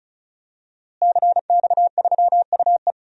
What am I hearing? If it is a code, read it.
Morse code, heard as a keyed tone, about 36 wpm, 700 Hz. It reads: CX3UE
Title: DXCC-100 Call Signs - 5 Characters - Repeated 3x - 35wpm